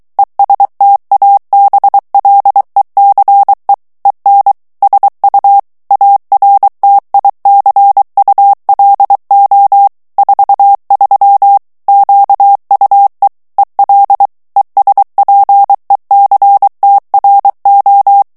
Morse code, heard as a keyed tone, about 23 wpm, 800 Hz. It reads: ESTABLECE EN SU ARTICULO 43 QUE EL ESPECTRO
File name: ESPAGNOL TEXTE EN CLAIR 2 VITESSE 22 MOTS MINUTES